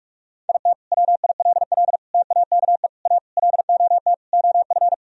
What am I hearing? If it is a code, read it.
Morse code, heard as a keyed tone, about 47 wpm, 700 Hz. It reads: IT WILL TAKE A LOT OF
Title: Sentences from Top 1000 Words - Review Part 4 - 45wpm